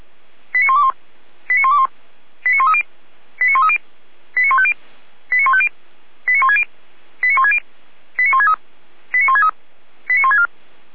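Five-tone selective call (ZVEI), heard as quick runs of five tones, tones of about 70 ms. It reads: 89121 89121 89130 89130 89150 89150 89160 89160 89163 89163 89164
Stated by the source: ZVEI five-tone call 89140 received by SDR, from the Feuerwehr Orlamünde audio server.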